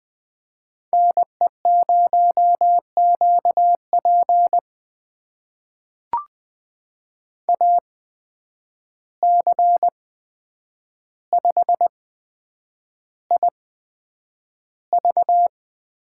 Morse code, heard as a keyed tone, about 20 wpm, spaced as for 9 wpm, 700 Hz. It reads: NE0QP AC5IV
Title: USA Call Signs - 8x Character Spacing - 20wpm